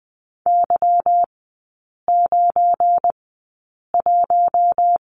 Morse code, heard as a keyed tone, about 20 wpm, spaced as for 8 wpm, 700 Hz. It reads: Y91
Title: 191 Lesson 31e - Letter v - Three Characters 20wpm